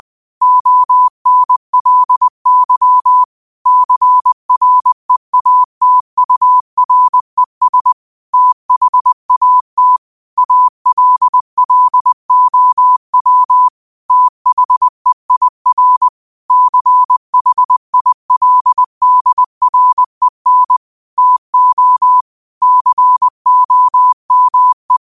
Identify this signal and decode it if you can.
Morse code, heard as a keyed tone, about 20 wpm, 1000 Hz. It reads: ONLY CREATURES THAT ALLOW THEIR CHILDREN TO COME